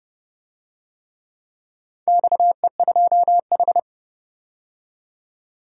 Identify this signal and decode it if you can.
Morse code, heard as a keyed tone, about 30 wpm, 700 Hz. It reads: XE2H
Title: DXCC-100 Call Signs - 4 Characters - 30wpm